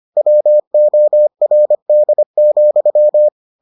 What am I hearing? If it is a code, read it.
Morse code, heard as a keyed tone, about 25 wpm, 600 Hz. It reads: WORD,